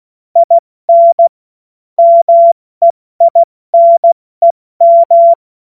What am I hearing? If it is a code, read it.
Morse code, heard as a keyed tone, about 16 wpm, 680 Hz. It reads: IN MEINEM
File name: Morsekurs / Witze / Witz 16. November 2021, 70 F80